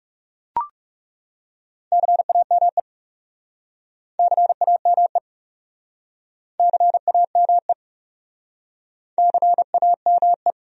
Morse code, heard as a keyed tone, about 42 wpm, 700 Hz. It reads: CAME CAME CAME CAME